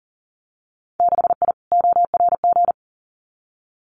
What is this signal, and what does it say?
Morse code, heard as a keyed tone, about 38 wpm, 700 Hz. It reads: 6I ORG
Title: Winter Field Day Exchanges Encore 40wpm